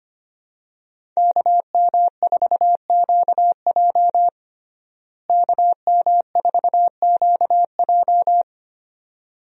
Morse code, heard as a keyed tone, about 25 wpm, 700 Hz. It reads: KM4QJ KM4QJ
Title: USA Call Signs - 5 Characters - Repeated 2x - 25wpm